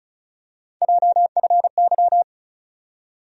Morse code, heard as a keyed tone, about 34 wpm, 700 Hz. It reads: JFY
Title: USA Call Sign Postfixes - 3 Letters 35wpm